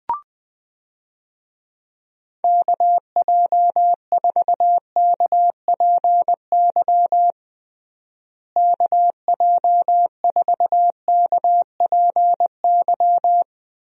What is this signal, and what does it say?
Morse code, heard as a keyed tone, about 20 wpm, 700 Hz. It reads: KJ4KPY KJ4KPY
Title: USA Call Signs - 6 Characters - Repeated 2x - 20wpm